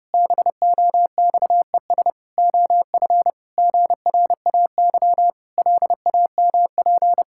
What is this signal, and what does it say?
Morse code, heard as a keyed tone, about 30 wpm, 700 Hz. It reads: BOXES OF GRAY LAMP
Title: Pangram Challenge v2 - 30wpm